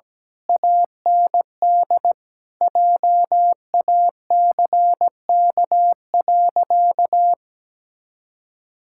Morse code, heard as a keyed tone, about 17 wpm, 700 Hz. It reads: AND JACK.
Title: Pangram Challenge v2 - 17wpm